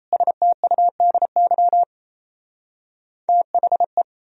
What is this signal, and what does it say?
Morse code, heard as a keyed tone, about 38 wpm, 700 Hz. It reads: STUDY THE